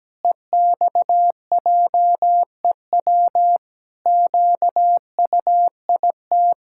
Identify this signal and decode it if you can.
Morse code, heard as a keyed tone, about 17 wpm, 700 Hz. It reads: EXJEW QUIT
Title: Pangram Challenge v2 - 17wpm